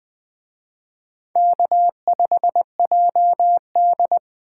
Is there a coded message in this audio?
Yes: Morse code from 1.4 to 4.2 s, heard as a keyed tone, about 20 wpm, 700 Hz. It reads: K5JD